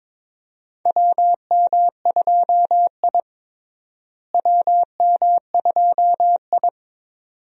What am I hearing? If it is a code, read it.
Morse code, heard as a keyed tone, about 22 wpm, 700 Hz. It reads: WM2I WM2I
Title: USA Call Signs - 4 Characters - Repeated 2x - 22wpm